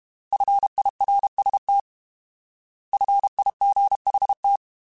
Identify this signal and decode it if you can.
Morse code, heard as a keyed tone, about 32 wpm, 800 Hz. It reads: FIRST FIGHT
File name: Alphabetical / Alphabetical mp3s / Fs @30